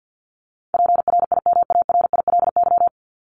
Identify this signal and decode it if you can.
Morse code, heard as a keyed tone, about 50 wpm, 700 Hz. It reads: PRIMARILY
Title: LY Suffix - Top 100 Words with Base Words 50wpm